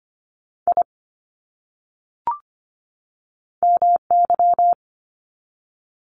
Morse code, heard as a keyed tone, about 24 wpm, 700 Hz. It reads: I MY
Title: Top 500 Words 25wpm